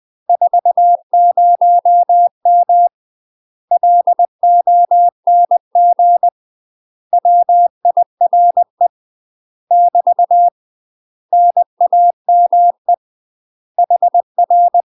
Morse code, heard as a keyed tone, about 20 wpm, 700 Hz. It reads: 40M LONG WIRE <BT> NAME HR
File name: UK MASTER QSOs TRAINING / 20WPM / UK MASTER QSOs 0068